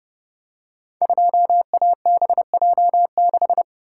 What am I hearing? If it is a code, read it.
Morse code, heard as a keyed tone, about 30 wpm, 700 Hz. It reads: 2ABJ6
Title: Code Groups - 5 Letter:Numbers - 30wpm